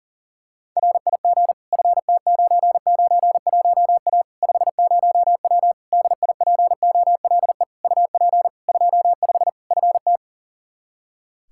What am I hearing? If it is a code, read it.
Morse code, heard as a keyed tone, about 40 wpm, 700 Hz. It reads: RIG FT991A 50W DIPOLE UP 25 FT
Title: Practice QSO - Exchange 2 - Info - 40wpm